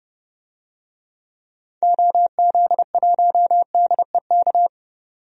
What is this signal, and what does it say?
Morse code, heard as a keyed tone, about 30 wpm, 700 Hz. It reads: OZ1DEK
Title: DXCC-100 Call Signs - 6 Characters - 30wpm